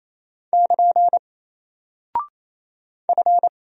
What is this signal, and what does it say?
Morse code, heard as a keyed tone, about 28 wpm, 700 Hz. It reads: <KN> F
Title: Single Character (Letter:Number:Punctuation:Prosign) Alt 28wpm